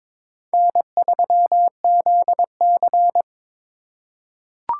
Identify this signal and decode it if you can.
Morse code, heard as a keyed tone, about 22 wpm, 700 Hz. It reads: N3ZC
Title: USA Call Signs - 4 Characters - 4x Character Spacing - 22wpm